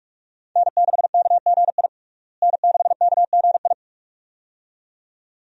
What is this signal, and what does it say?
Morse code, heard as a keyed tone, about 42 wpm, 700 Hz. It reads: N6KGI N6KGI
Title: USA Call Signs - 5 Characters - Repeated 2x - 45wpm